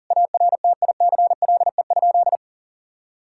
Morse code, heard as a keyed tone, about 40 wpm, 700 Hz. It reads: ARTICLE?
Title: Sentences from Top 1000 Words - Review Part 5 - 40wpm